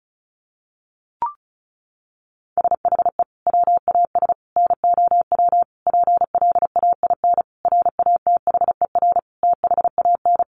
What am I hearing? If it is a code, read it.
Morse code, heard as a keyed tone, about 34 wpm, 700 Hz. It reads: SHE WAS NOW PLAIN RATHER THAN
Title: Sentences from Top 1000 Words 35wpm